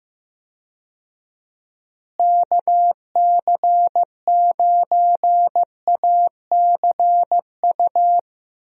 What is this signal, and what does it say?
Morse code, heard as a keyed tone, about 15 wpm, 700 Hz. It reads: KC9ACU
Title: USA Call Signs - 6 Characters - Repeated 3x - 15wpm